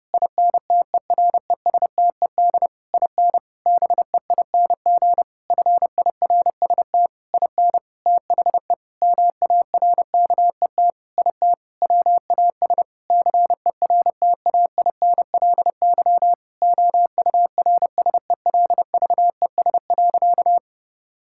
Morse code, heard as a keyed tone, about 32 wpm, 700 Hz. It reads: INTERESTED IN BEING FIRST IN THE MARKET IT WAS CERTAINLY OURSELVES.
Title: Sentences from Top 1600 Words 30wpm